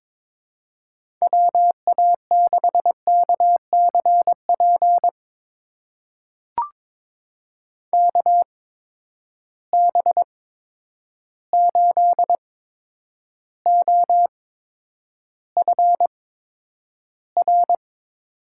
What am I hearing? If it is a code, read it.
Morse code, heard as a keyed tone, about 21 wpm, spaced as for 9 wpm, 700 Hz. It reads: WA6KCP KB8OFR
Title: USA Call Signs - 6 Characters - 8x Character Spacing - 22wpm